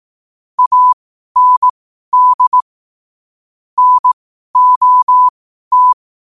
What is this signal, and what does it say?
Morse code, heard as a keyed tone, about 18 wpm, spaced as for 12 wpm, 1000 Hz. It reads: AND NOT
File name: qotdfiles / 12wpm